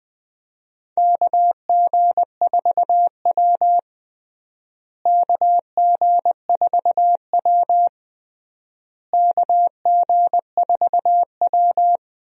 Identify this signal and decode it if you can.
Morse code, heard as a keyed tone, about 20 wpm, 700 Hz. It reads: KG4W KG4W KG4W
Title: USA Call Signs - 4 Characters - Repeated 3x - 20wpm